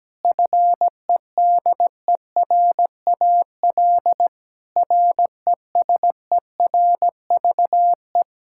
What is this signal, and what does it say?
Morse code, heard as a keyed tone, about 17 wpm, 700 Hz. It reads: FEDERAL RESERVE
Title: Sentences from Top 1000 Words - Review Part 6 - 17wpm